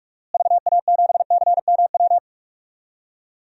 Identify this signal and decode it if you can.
Morse code, heard as a keyed tone, about 46 wpm, 700 Hz. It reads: VA7CMW
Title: DXCC-100 Call Signs - 6 Characters - Repeated 3x - 45wpm